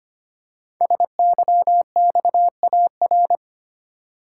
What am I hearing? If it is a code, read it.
Morse code, heard as a keyed tone, about 25 wpm, 700 Hz. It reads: SYXAR
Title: Code Groups - 5 Letters - 25wpm